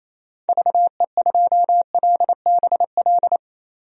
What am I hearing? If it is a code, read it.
Morse code, heard as a keyed tone, about 27 wpm, 700 Hz. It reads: VE2LBL
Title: DXCC-100 Call Signs - 6 Characters - Repeated 3x - 28wpm